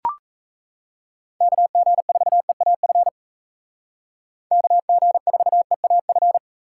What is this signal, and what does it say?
Morse code, heard as a keyed tone, about 36 wpm, 700 Hz. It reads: KG4EAF KG4EAF